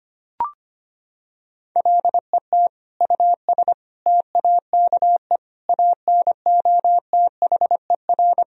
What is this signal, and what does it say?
Morse code, heard as a keyed tone, about 25 wpm, 700 Hz. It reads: LET US TAKE ANOTHER
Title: Sentences from Top 1000 Words - Review Part 2 - 25wpm